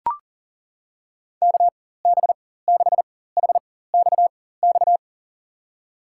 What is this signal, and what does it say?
Morse code, heard as a keyed tone, about 40 wpm, spaced as for 19 wpm, 700 Hz. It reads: KB6HXX
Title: USA Call Signs - 6 Characters - 4x Character Spacing - 40wpm